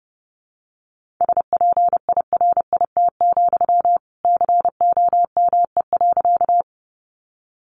Morse code, heard as a keyed tone, about 30 wpm, 700 Hz. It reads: SPIRIT, COME.
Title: Sentences from Top 1400 Words 30wpm